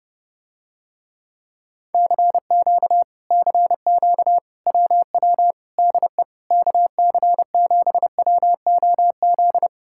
Morse code, heard as a keyed tone, about 30 wpm, 700 Hz. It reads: CQ CQ WW DE KC7WOZ